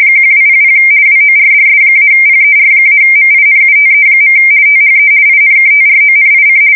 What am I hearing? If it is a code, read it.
RTTY, about 50 baud. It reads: 1200 WORDS-PER-MINUTE. THE INKTRONIC
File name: fsk170-50B / teletype-corp